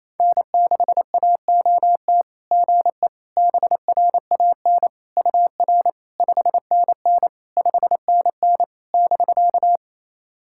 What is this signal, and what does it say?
Morse code, heard as a keyed tone, about 28 wpm, 690 Hz. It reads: N6AOT GE BRAN UR 5NN 5NN <BK>
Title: SOTA QSO Practice 28wpm